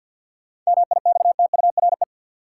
Morse code, heard as a keyed tone, about 47 wpm, 700 Hz. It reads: MIXTURE
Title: Sentences from Top 1500 Words - New Word Drill - 50wpm